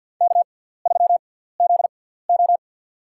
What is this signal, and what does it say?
Morse code, heard as a keyed tone, about 53 wpm, spaced as for 17 wpm, 690 Hz. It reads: K3ZO
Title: USA Call Signs - 4 Characters - 6x Character Spacing - 50wpm